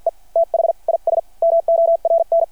Morse code, heard as a keyed tone, about 53 wpm, spaced as for 43 wpm, 690 Hz. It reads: E THIS MORN
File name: archived / CRSnet